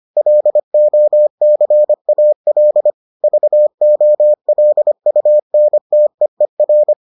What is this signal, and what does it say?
Morse code, heard as a keyed tone, about 25 wpm, 600 Hz. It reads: LOCAL VOLUNTEER